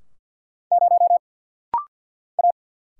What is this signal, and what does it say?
Morse code, heard as a keyed tone, about 50 wpm, 700 Hz. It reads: 0 A
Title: Single Letter-Number - Rapid-Fire 50wpm